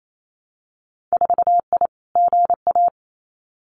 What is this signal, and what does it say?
Morse code, heard as a keyed tone, about 28 wpm, 690 Hz. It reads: 4I GA
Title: Winter Field Day Exchanges Encore 28wpm